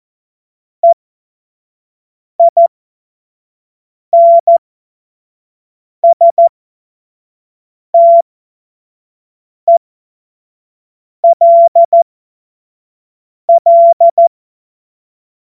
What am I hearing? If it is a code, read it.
Morse code, heard as a keyed tone, about 14 wpm, spaced as for 5 wpm, 680 Hz. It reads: EINSTELL
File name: Morsekurs / Witze / Witz 23. August 2022 25 F70